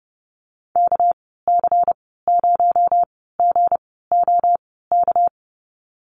Morse code, heard as a keyed tone, about 30 wpm, spaced as for 17 wpm, 700 Hz. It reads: KC0GOK
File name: Lesson 37g - Letter x - Call Signs 30wpm